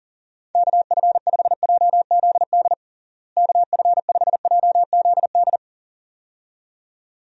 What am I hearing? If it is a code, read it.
Morse code, heard as a keyed tone, about 40 wpm, 700 Hz. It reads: KF5JZD KF5JZD
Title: USA Call Signs - 6 Characters - Repeated 2x - 40wpm